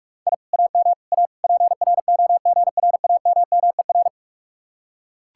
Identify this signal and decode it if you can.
Morse code, heard as a keyed tone, about 44 wpm, 700 Hz. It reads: I AM A PROGRAMMER